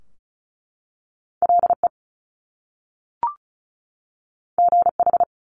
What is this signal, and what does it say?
Morse code, heard as a keyed tone, about 36 wpm, 700 Hz. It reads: LE GH
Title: Top Two Letter Combinations in Words 35wpm